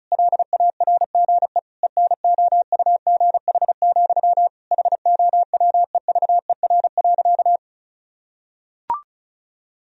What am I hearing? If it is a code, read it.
Morse code, heard as a keyed tone, about 36 wpm, 700 Hz. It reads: LARGE ENOUGH, HOWEVER.
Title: Sentences from Top 1000 Words - Review Part 1 - 35wpm